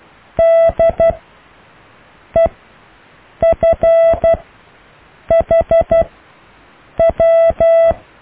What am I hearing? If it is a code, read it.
Morse code, heard as a keyed tone, about 12 wpm, spaced as for 6 wpm, 660 Hz. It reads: DEFHW
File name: cw / EKTPV+BFHNW